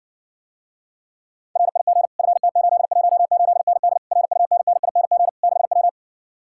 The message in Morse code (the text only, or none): RIG FT818ND RUNNING 6W